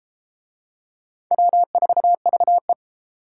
Morse code, heard as a keyed tone, about 33 wpm, 700 Hz. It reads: W4VE